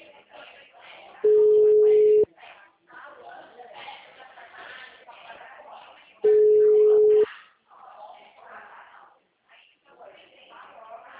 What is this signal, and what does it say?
Ringback tone (Europe, 425 Hz), with speech.